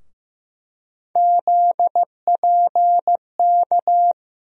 Morse code, heard as a keyed tone, about 15 wpm, 700 Hz. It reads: ZPK
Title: USA Call Sign Postfixes - 3 Letters 15wpm